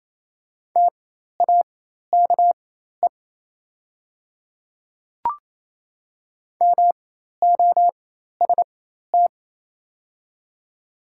Morse code, heard as a keyed tone, about 27 wpm, spaced as for 13 wpm, 690 Hz. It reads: TAKE MOST
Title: 4x Character Spacing - Top 200 Words - 28wpm